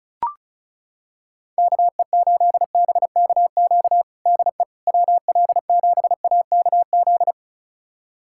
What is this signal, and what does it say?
Morse code, heard as a keyed tone, about 36 wpm, 700 Hz. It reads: KE8BKQ DE WL7AKZ
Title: USA Callsign DE Callsign - 6 Characters - 35wpm